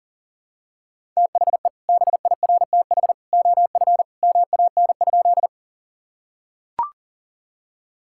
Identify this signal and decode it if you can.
Morse code, heard as a keyed tone, about 37 wpm, 700 Hz. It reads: THE BIRTH OF MAN?
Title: Sentences from Top 1000 Words - Review Part 5 - 40wpm